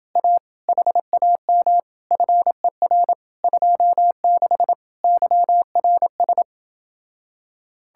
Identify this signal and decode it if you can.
Morse code, heard as a keyed tone, about 28 wpm, 700 Hz. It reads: A HAM FER 26 YRS